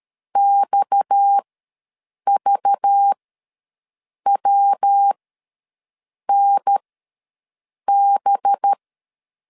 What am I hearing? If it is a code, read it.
Morse code, heard as a keyed tone, about 13 wpm, spaced as for 6 wpm, 800 Hz. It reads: XVWNB
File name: LEZIONI / 7 GF9Z / 7 GF9 ws 12wpm 3344